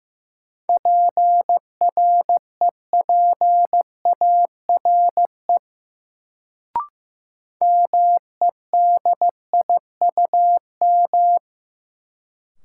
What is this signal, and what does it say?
Morse code, heard as a keyed tone, about 15 wpm, 700 Hz. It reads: PREPARE MEDIUM
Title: Sentences from Top 900 Words - New Word Drill - 15wpm